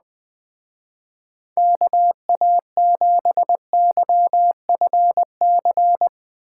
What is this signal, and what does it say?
Morse code, heard as a keyed tone, about 20 wpm, 700 Hz. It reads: KA7YFC
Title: Speed Racing - USA 2x3 Call Signs 17wpm